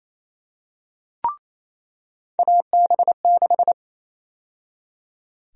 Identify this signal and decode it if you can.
Morse code, heard as a keyed tone, about 27 wpm, 700 Hz. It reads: AB6
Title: USA Call Sign Prefixes - 28wpm